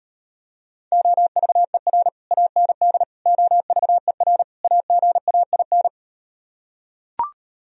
Morse code, heard as a keyed tone, about 39 wpm, 700 Hz. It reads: OVER AND OVER AGAIN